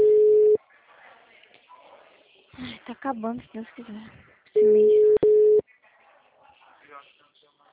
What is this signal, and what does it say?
Ringback tone (Europe, 425 Hz), with speech.